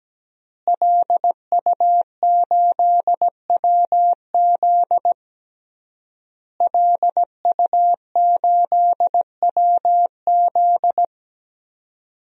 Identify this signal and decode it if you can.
Morse code, heard as a keyed tone, about 17 wpm, 700 Hz. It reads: LU8WZ LU8WZ
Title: DXCC-100 Call Signs - 5 Characters - Repeated 3x - 17wpm